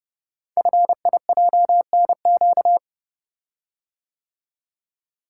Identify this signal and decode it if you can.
Morse code, heard as a keyed tone, about 30 wpm, 700 Hz. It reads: FIJNQ
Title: Code Groups - 5 Letters - 30wpm